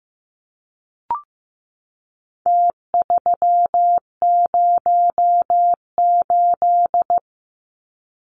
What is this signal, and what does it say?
Morse code, heard as a keyed tone, about 15 wpm, 700 Hz. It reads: T308
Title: Numbers - Contest Style - 4-Digit 15wpm